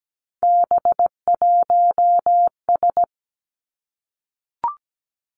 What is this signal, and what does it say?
Morse code, heard as a keyed tone, about 17 wpm, 700 Hz. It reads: B1S
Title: Three Challenging Letter-Number Combinations 17wpm